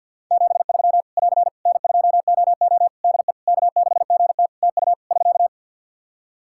Morse code, heard as a keyed tone, about 50 wpm, 700 Hz. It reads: 73 <AR> N2OO DE K6GT TU <SK>
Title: Practice QSO - Exchange 4 - Ending - 50wpm